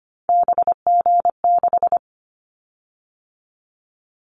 Morse code, heard as a keyed tone, about 25 wpm, 700 Hz. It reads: BG6
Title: Callsign Prefixes for Top 100 DXCC - 25wpm